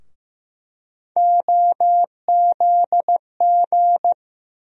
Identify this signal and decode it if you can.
Morse code, heard as a keyed tone, about 15 wpm, 700 Hz. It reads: OZG
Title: USA Call Sign Postfixes - 3 Letters 15wpm